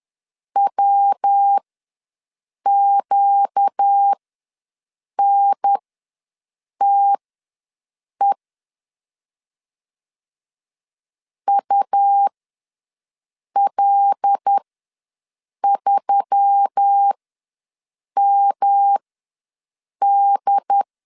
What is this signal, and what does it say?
Morse code, heard as a keyed tone, about 10 wpm, spaced as for 6 wpm, 800 Hz. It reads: WQNTE UL3MD
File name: LEZIONI / 5 LBVC / 5 L ws 10wpm 3636